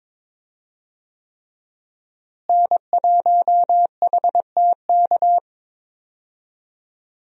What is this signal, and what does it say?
Morse code, heard as a keyed tone, about 21 wpm, 690 Hz. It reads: N1HTK
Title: USA Call Signs - 5 Characters - Repeated 3x - 22wpm